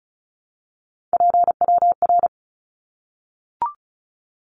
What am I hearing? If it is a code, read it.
Morse code, heard as a keyed tone, about 34 wpm, 700 Hz. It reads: PWR
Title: Abbreviations 35wpm